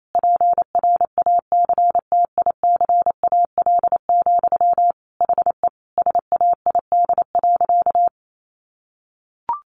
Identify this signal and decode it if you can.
Morse code, heard as a keyed tone, about 28 wpm, 700 Hz. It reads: PRACTICAL, HE SAID.
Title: Sentences from Top 1400 Words 28wpm